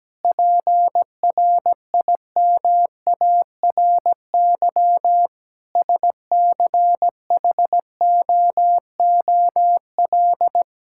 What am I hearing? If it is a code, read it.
Morse code, heard as a keyed tone, about 17 wpm, 700 Hz. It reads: PRIMARY SCHOOL
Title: Sentences from Top 1000 Words - Review Part 5 - 17wpm